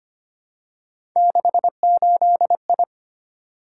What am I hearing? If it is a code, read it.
Morse code, heard as a keyed tone, about 25 wpm, 700 Hz. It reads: 68I
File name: Three Letter-Number Combinations - 25wpm